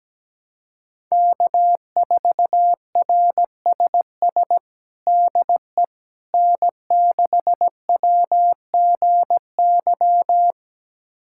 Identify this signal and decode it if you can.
Morse code, heard as a keyed tone, about 17 wpm, 700 Hz. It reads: K4RSS DE N6WGY
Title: USA Callsign DE Callsign 17wpm